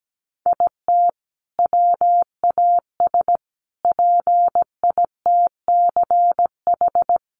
Morse code, heard as a keyed tone, about 17 wpm, 700 Hz. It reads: IT WAS PITCH
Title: Sentences from Top 1300 Words 17wpm